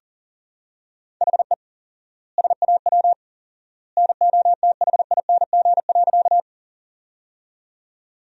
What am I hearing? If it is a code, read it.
Morse code, heard as a keyed tone, about 40 wpm, 700 Hz. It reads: HE SAW NOTHING.
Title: 4x Word-Spacing - Sentences from Top 500 Words - 40wpm